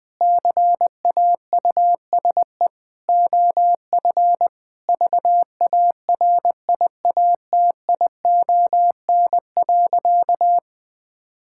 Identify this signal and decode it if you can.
Morse code, heard as a keyed tone, about 20 wpm, 700 Hz. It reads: CAUSE OF VARIATION.
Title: Sentences from Top 1600 Words 20wpm